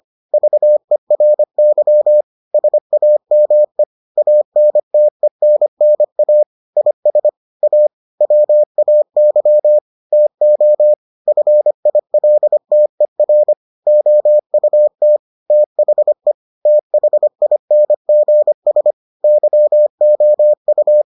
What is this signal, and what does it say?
Morse code, heard as a keyed tone, about 25 wpm, 600 Hz. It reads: VERY SAME ANTENNA IS A WAY TO FILTER OUT THE THINGS YOU